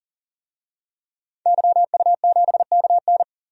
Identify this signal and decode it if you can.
Morse code, heard as a keyed tone, about 39 wpm, 700 Hz. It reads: YU7KN
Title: DXCC-100 Call Signs - 5 Characters - 40wpm